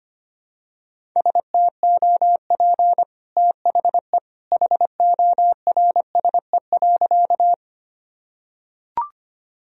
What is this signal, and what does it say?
Morse code, heard as a keyed tone, about 25 wpm, 700 Hz. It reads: STOP THE HORSE.
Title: Sentences from Top 1000 Words - Review Part 1 - 25wpm